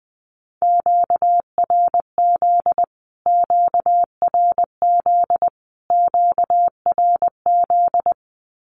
Morse code, heard as a keyed tone, about 20 wpm, 700 Hz. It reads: QRZ QRZ QRZ
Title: Q-Codes Repeated 3x 20wpm